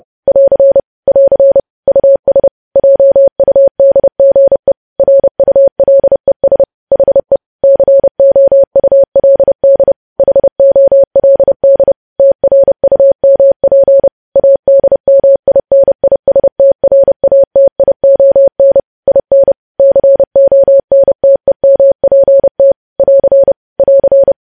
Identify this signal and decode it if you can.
Morse code, heard as a keyed tone, about 30 wpm, 550 Hz. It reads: <AR> <AR> US JUDGE RULES HE COULD HOLD TRUMP ADMINISTRATION IN CONTEMPT <AR> <AR>
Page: news headlines in morse code at 30 wpm